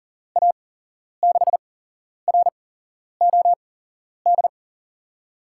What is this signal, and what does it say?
Morse code, heard as a keyed tone, about 39 wpm, spaced as for 11 wpm, 700 Hz. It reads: A6ROD
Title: USA Call Signs - 6 Characters - 8x Character Spacing - 40wpm